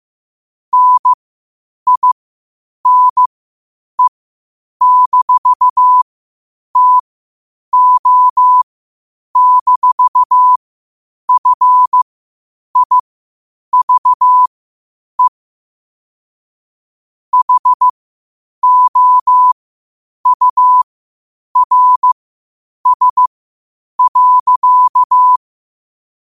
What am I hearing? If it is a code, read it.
Morse code, heard as a keyed tone, about 15 wpm, spaced as for 8 wpm, 1000 Hz. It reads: NINE-TO-FIVE HOURS.
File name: qotdfiles / 8wpm